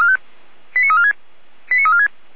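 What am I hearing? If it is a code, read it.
Five-tone selective call (ZVEI), heard as quick runs of five tones, tones of about 70 ms. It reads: 89346 89346
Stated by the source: ZVEI five-tone call received by SDR.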